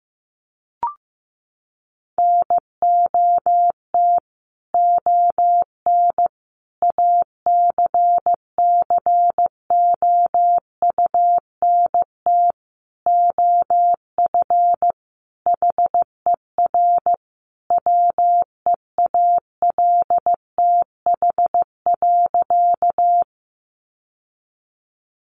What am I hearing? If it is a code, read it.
Morse code, heard as a keyed tone, about 15 wpm, 700 Hz. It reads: NOT ON ACCOUNT OF HER WEALTH.
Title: Sentences from Top 1200 Words 15wpm